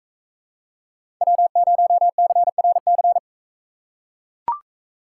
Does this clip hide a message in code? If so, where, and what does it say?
Yes: Morse code from 1.2 to 3.2 s, heard as a keyed tone, about 44 wpm, 700 Hz. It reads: W0CRC